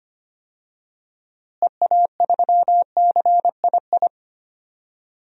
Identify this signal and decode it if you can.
Morse code, heard as a keyed tone, about 25 wpm, 700 Hz. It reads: EA3CII
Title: DXCC-100 Call Signs - 6 Characters - Repeated 2x - 25wpm